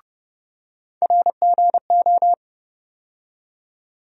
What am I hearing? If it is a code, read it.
Morse code, heard as a keyed tone, about 30 wpm, 700 Hz. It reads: RGO
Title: USA Call Sign Postfixes - 3 Letters 30wpm